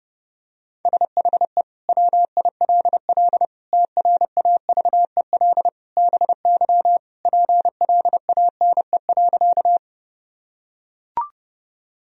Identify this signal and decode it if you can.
Morse code, heard as a keyed tone, about 32 wpm, 700 Hz. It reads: SHE WILL TRAVEL BY PLANE.
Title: Sentences from Top 1000 Words - Review Part 3 - 30wpm